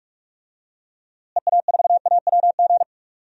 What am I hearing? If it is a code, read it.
Morse code, heard as a keyed tone, about 46 wpm, 700 Hz. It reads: EA4AWG
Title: DXCC-100 Call Signs - 6 Characters - Repeated 3x - 45wpm